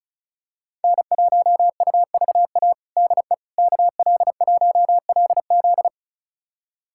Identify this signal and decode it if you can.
Morse code, heard as a keyed tone, about 36 wpm, 700 Hz. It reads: N1UVA DE KL1LZ
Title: USA Callsign DE Callsign - 5 Characters - 35wpm